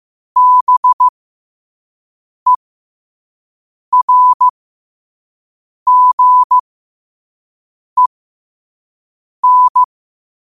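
Morse code, heard as a keyed tone, about 15 wpm, spaced as for 5 wpm, 1000 Hz. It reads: BERGEN
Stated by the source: Morse practice at 5 wpm.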